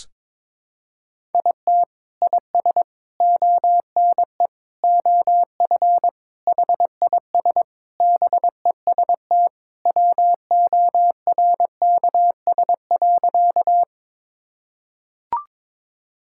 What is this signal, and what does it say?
Morse code, heard as a keyed tone, about 21 wpm, 700 Hz. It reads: IT IS ONE OF HIS BEST WORKS.
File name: practice / 22wpm / Sentences with 3-Word Phrases - 22wpm